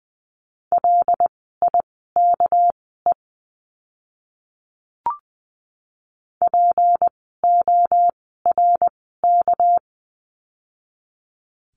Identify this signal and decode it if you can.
Morse code, heard as a keyed tone, about 20 wpm, spaced as for 14 wpm, 700 Hz. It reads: LIKE PORK
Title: 199 Lesson 32f - Letter k - Words 20wpm